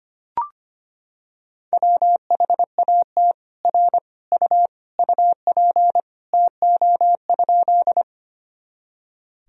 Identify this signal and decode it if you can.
Morse code, heard as a keyed tone, about 25 wpm, 700 Hz. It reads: WHAT R U UP TO?
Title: Sentences from Top 100 Words and Abbreviations 25wpm